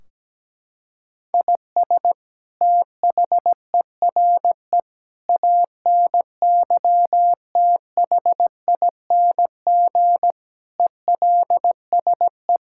Morse code, heard as a keyed tone, about 17 wpm, 700 Hz. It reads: IS THERE ANYTHING ELSE
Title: Sentences from Top 1600 Words 17wpm